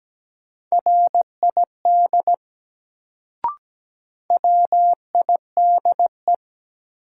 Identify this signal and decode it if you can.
Morse code, heard as a keyed tone, about 17 wpm, 700 Hz. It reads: RID WIDE
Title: Sets of 4 Words - New Word Drill - 17wpm